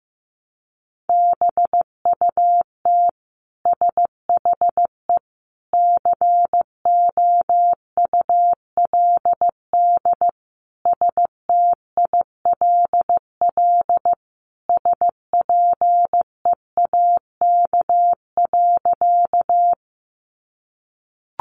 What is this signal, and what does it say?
Morse code, heard as a keyed tone, about 15 wpm, 700 Hz. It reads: BUT SHE COULD STILL SPEAK.
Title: Sentences from Top 1100 Words 15wpm